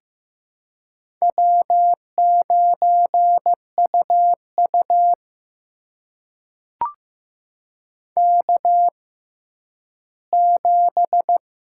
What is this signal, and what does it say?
Morse code, heard as a keyed tone, about 15 wpm, spaced as for 10 wpm, 700 Hz. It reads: W9UU K7
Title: USA Call Signs - 4 Characters - 6x Character Spacing - 15wpm